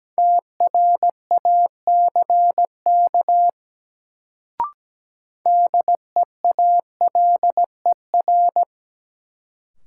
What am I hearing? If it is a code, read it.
Morse code, heard as a keyed tone, about 17 wpm, 700 Hz. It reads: TRACK DEALER
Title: Sentences from Top 1100 Words - New Word Drill - 17wpm